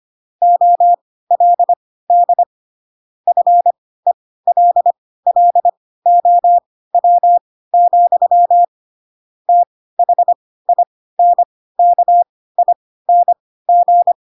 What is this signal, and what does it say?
Morse code, heard as a keyed tone, about 25 wpm, spaced as for 16 wpm, 700 Hz. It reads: OLD FELLOW, THINKING
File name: Pinocchio / Pinocchio ch6 10 25 ik1hge